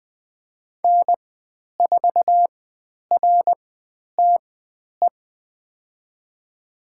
Morse code, heard as a keyed tone, about 20 wpm, spaced as for 10 wpm, 700 Hz. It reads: N4RTE
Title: 102 Lesson 17g - Letter u - Call Signs 20wpm